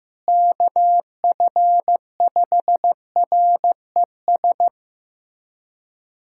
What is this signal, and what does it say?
Morse code, heard as a keyed tone, about 15 wpm, 700 Hz. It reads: KF5RES
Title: USA Callsign DE Callsign 15wpm